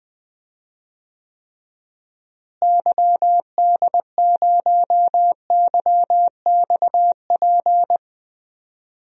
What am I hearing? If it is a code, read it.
Morse code, heard as a keyed tone, about 20 wpm, 700 Hz. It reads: YD0YXP
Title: DXCC-100 Call Signs - 6 Characters - Repeated 2x - 20wpm